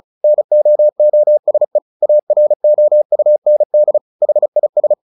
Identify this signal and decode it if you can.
Morse code, heard as a keyed tone, about 35 wpm, 600 Hz. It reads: NOOSE AROUND HIS
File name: the old man and the sea 2 35wpm